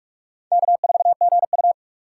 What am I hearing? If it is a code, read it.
Morse code, heard as a keyed tone, about 42 wpm, 700 Hz. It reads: K4GU
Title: USA Call Signs - 4x Character Spacing - 45wpm